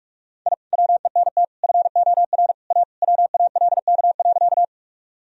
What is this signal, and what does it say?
Morse code, heard as a keyed tone, about 43 wpm, 700 Hz. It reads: I WENT FOR A WALK.
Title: Sentences from Top 1000 Words - Review Part 1 - 45wpm